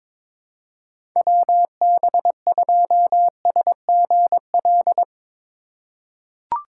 Morse code, heard as a keyed tone, about 22 wpm, 700 Hz. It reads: WB2SGL
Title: USA Call Signs - 6 Characters - 2x Character Spacing - 22wpm